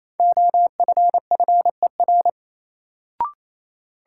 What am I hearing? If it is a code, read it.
Morse code, heard as a keyed tone, about 28 wpm, 700 Hz. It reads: OFFER
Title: Sentences from Top 900 Words - New Word Drill - 28wpm